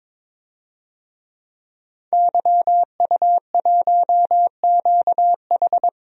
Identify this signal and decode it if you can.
Morse code, heard as a keyed tone, about 22 wpm, 700 Hz. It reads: YU1QH
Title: DXCC-100 Call Signs - 5 Characters - 22wpm